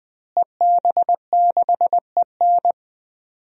Morse code, heard as a keyed tone, about 20 wpm, 700 Hz. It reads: EB6EN
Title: DXCC-100 Call Signs - 5 Characters - Repeated 2x - 20wpm